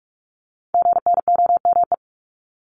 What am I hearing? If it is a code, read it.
Morse code, heard as a keyed tone, about 46 wpm, 690 Hz. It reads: GNOME
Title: Multi-letter Phonemes 45wpm